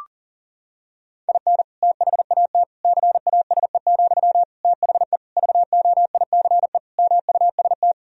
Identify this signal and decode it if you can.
Morse code, heard as a keyed tone, about 40 wpm, 700 Hz. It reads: IN THAT CASE, THE VOICE MUST
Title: Sentences from Top 1000 Words - Review Part 2 - 40wpm